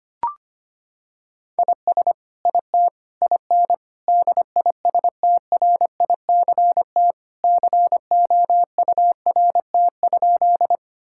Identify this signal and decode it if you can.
Morse code, heard as a keyed tone, about 25 wpm, 700 Hz. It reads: IS IT IN DISTRICT COURT?
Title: Sentences from Top 1600 Words 25wpm